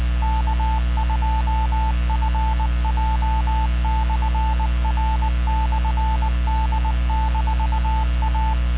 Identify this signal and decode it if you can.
Morse code, heard as a keyed tone, about 20 wpm, 880 Hz, with radio noise. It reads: K2FJ/R/D-A